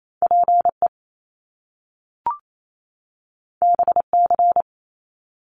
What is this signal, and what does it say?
Morse code, heard as a keyed tone, about 28 wpm, 700 Hz. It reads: PE BC
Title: Canadian Provinces and Territories 28wpm